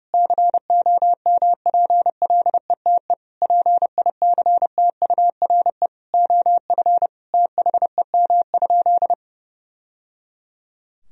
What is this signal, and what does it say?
Morse code, heard as a keyed tone, about 32 wpm, 700 Hz. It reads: COMPLETE PICTURE OF THEM?
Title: Sentences from Top 1000 Words - Review Part 2 - 30wpm